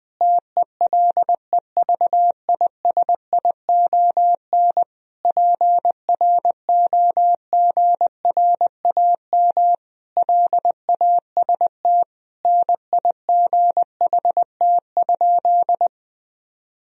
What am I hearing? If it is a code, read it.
Morse code, heard as a keyed tone, about 21 wpm, 700 Hz. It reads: TELEVISION PROGRAM LAST NIGHT?
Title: Sentences from Top 1000 Words - Review Part 4 - 20wpm